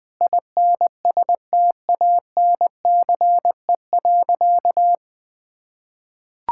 Morse code, heard as a keyed tone, about 20 wpm, 700 Hz. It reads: INSTANCE.
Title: Sentences from Top 1000 Words - Review Part 5 - 20wpm